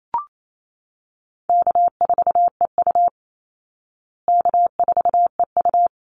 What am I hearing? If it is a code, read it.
Morse code, heard as a keyed tone, about 28 wpm, 700 Hz. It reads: K4EU K4EU
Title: Easy Call Signs Repeated 3x - 28wpm